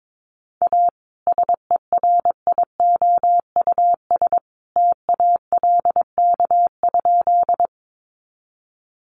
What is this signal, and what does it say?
Morse code, heard as a keyed tone, about 22 wpm, 700 Hz. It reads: A SERIOUS TALK?
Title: Sentences from Top 1100 Words 22wpm